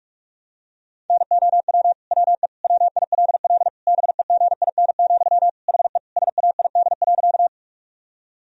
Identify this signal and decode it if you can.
Morse code, heard as a keyed tone, about 42 wpm, 700 Hz. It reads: NOW WE WILL BEGIN, HE SAID.
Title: Sentences from Top 1000 Words - Review Part 1 - 45wpm